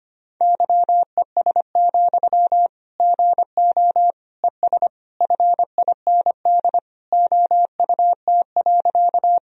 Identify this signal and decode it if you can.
Morse code, heard as a keyed tone, about 25 wpm, 700 Hz. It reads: YES, GO ES FIND OUT.
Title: Sentences from Top 500 Words and Abbreviations 25wpm